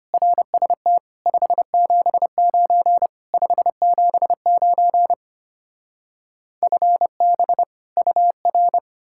Morse code, heard as a keyed tone, about 33 wpm, 700 Hz. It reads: RST 579 579 FB UR